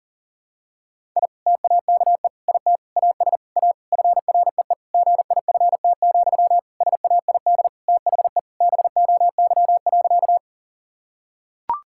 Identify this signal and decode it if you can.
Morse code, heard as a keyed tone, about 37 wpm, 700 Hz. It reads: I TAKE IT AS A FREE GIFT, SAID THE BOY.
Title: Sentences from Top 1000 Words - Review Part 6 - 40wpm